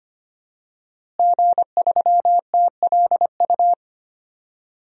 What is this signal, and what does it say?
Morse code, heard as a keyed tone, about 25 wpm, 700 Hz. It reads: G3TLU
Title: DXCC-100 Call Signs - 5 Characters - Repeated 3x - 25wpm